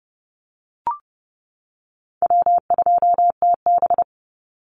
Morse code, heard as a keyed tone, about 30 wpm, 700 Hz. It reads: W2TB
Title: Easy Call Signs - 30wpm